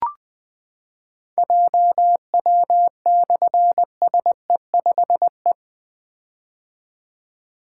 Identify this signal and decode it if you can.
Morse code, heard as a keyed tone, about 20 wpm, 700 Hz. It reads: JW/SE5E
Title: Difficult Call Signs 20wpm